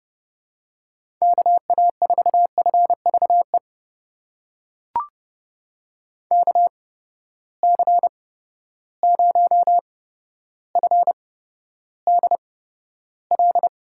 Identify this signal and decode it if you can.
Morse code, heard as a keyed tone, about 29 wpm, spaced as for 13 wpm, 700 Hz. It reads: KA4FVE KC0FDL
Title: USA Call Signs - 6 Characters - 8x Character Spacing - 30wpm